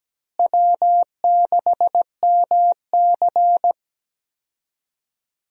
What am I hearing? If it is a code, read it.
Morse code, heard as a keyed tone, about 17 wpm, 700 Hz. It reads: W6MC
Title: USA Call Signs - 4 Characters - Repeated 2x - 17wpm